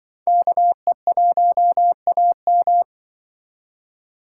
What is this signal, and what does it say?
Morse code, heard as a keyed tone, about 23 wpm, 700 Hz. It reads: KE1AM
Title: Speed Racing - USA 2x2 Call Signs 20wpm